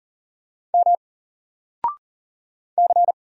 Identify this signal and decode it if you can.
Morse code, heard as a keyed tone, about 40 wpm, 700 Hz. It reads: M C
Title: Single Letter-Number 40wpm